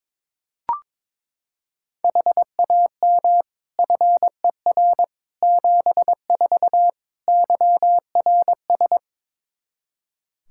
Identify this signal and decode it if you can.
Morse code, heard as a keyed tone, about 22 wpm, 700 Hz. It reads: HAM FER 74 YRS
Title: Practice QSO - Exchange 3 - More Info - 22wpm